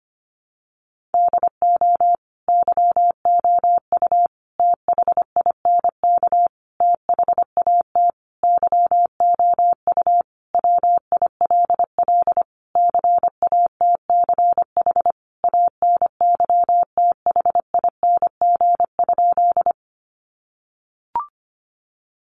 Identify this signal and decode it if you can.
Morse code, heard as a keyed tone, about 25 wpm, 700 Hz. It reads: DO YOU THINK THAT YOU WILL CATCH ANYTHING?
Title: Sentences from Top 1500 Words 25wpm